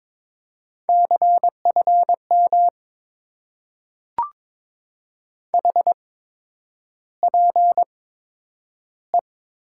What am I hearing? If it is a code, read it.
Morse code, heard as a keyed tone, about 22 wpm, spaced as for 9 wpm, 700 Hz. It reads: CFM HPE
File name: Top 90 QSO Elements - 8x Character Spacing - 22wpm